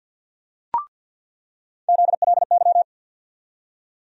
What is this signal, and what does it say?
Morse code, heard as a keyed tone, about 48 wpm, 700 Hz. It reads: ZLY
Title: USA Call Sign Postfixes - 3 Letters 50wpm